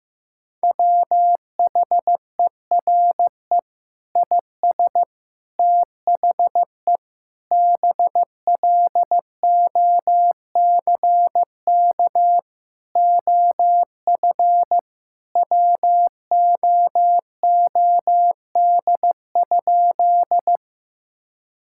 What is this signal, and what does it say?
Morse code, heard as a keyed tone, about 16 wpm, 700 Hz. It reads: WHERE IS THE BLOCK OF WOOD?
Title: Sentences from Top 1600 Words 15wpm